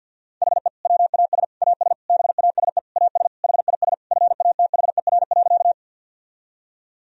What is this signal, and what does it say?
Morse code, heard as a keyed tone, about 52 wpm, 700 Hz. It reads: HE WAS AS BASE AS HIS FATHER.
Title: Sentences from Top 1000 Words - Review Part 1 - 50wpm